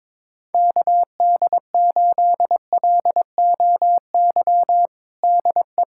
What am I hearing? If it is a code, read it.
Morse code, heard as a keyed tone, about 22 wpm, 700 Hz. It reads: KD8LOY DE
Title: USA Callsign DE Callsign - 6 Characters - 22wpm